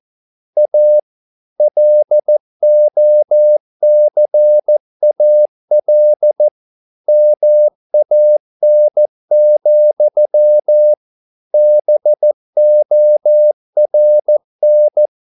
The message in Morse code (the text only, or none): A LOCAL MAN, BORN